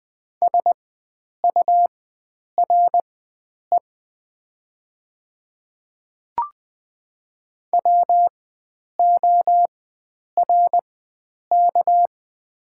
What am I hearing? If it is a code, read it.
Morse code, heard as a keyed tone, about 20 wpm, spaced as for 9 wpm, 700 Hz. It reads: SURE WORK
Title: 4x Character Spacing - Top 300 Words - 20wpm